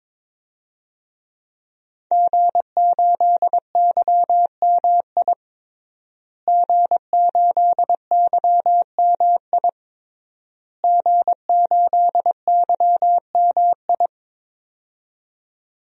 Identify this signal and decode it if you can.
Morse code, heard as a keyed tone, about 22 wpm, 700 Hz. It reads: G8YMI G8YMI G8YMI
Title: DXCC-100 Call Signs - 5 Characters - Repeated 3x - 22wpm